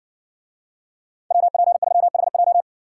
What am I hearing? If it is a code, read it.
Morse code, heard as a keyed tone, about 60 wpm, 700 Hz. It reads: WP3HJ